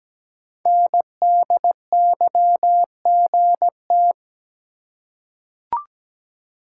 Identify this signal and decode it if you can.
Morse code, heard as a keyed tone, about 17 wpm, 700 Hz. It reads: NDYGT
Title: Code Groups - 5 Letter:Numbers - 17wpm